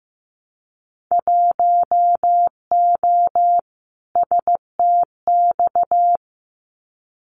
Morse code, heard as a keyed tone, about 15 wpm, 700 Hz. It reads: 1O STX
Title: Winter Field Day Exchanges 15wpm